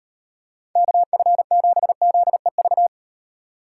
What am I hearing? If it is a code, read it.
Morse code, heard as a keyed tone, about 36 wpm, 700 Hz. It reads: KF7ZEV